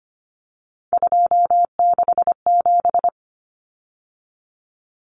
Morse code, heard as a keyed tone, about 25 wpm, 690 Hz. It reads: 267
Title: Numbers - 3-Digit 25wpm